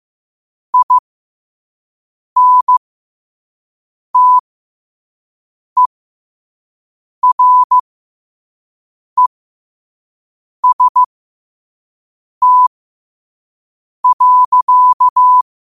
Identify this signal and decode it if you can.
Morse code, heard as a keyed tone, about 15 wpm, spaced as for 5 wpm, 1000 Hz. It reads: INTEREST.
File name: qotdfiles / 5wpm